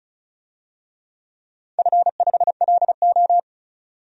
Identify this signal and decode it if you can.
Morse code, heard as a keyed tone, about 34 wpm, 700 Hz. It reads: F5LO
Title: DXCC-100 Call Signs - 4 Characters - 35wpm